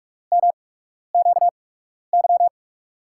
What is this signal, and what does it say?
Morse code, heard as a keyed tone, about 46 wpm, spaced as for 12 wpm, 700 Hz. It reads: MQY
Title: USA Call Signs - 5 Characters - 8x Character Spacing - 45wpm